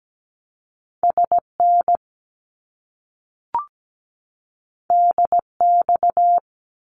Morse code, heard as a keyed tone, about 17 wpm, 700 Hz. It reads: SN DX
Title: Abbreviations Repeated 3x - 17wpm